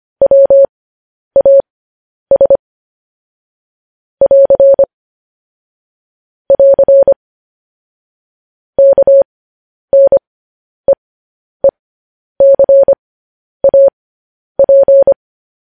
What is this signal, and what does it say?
Morse code, heard as a keyed tone, about 25 wpm, spaced as for 10 wpm, 550 Hz. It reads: WAS <AR> <AR> KNEECAP
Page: news headlines in morse code at 10 wpm